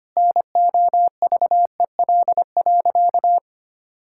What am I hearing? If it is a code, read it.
Morse code, heard as a keyed tone, about 25 wpm, 700 Hz. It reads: NOVEL.
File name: practice / 25wpm / Sentences with 4-Word Phrases - 25wpm